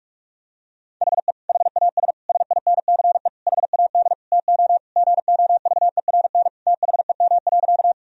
Morse code, heard as a keyed tone, about 42 wpm, 700 Hz. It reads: HE HAS SINCE HAD TO GOVERN THEM.
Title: Sentences from Top 1000 Words - Review Part 2 - 45wpm